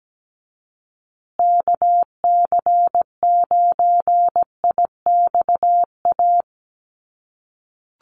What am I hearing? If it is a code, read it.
Morse code, heard as a keyed tone, about 17 wpm, 700 Hz. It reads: KC9IXA
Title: USA Call Signs Repeated 3x 17wpm